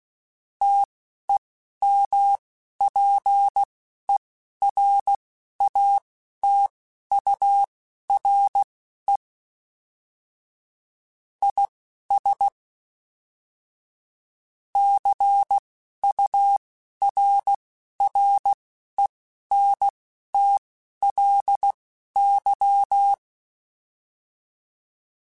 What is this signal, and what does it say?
Morse code, heard as a keyed tone, about 16 wpm, spaced as for 12 wpm, 800 Hz. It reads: TEMPERATURE IS CURRENTLY
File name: Weather Sets / Weather Sets-mp3s / Weather 5 12 @10